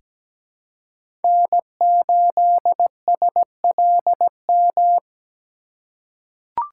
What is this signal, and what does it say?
Morse code, heard as a keyed tone, about 17 wpm, 700 Hz. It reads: N8SLM
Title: USA Call Signs - 5 Characters - 4x Character Spacing - 17wpm